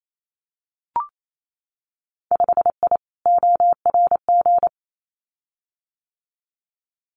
Morse code, heard as a keyed tone, about 28 wpm, 700 Hz. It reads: 5I ORG
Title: Winter Field Day Exchanges Encore 28wpm